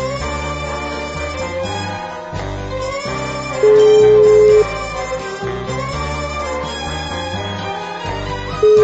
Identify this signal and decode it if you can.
Ringback tone (Europe, 425 Hz).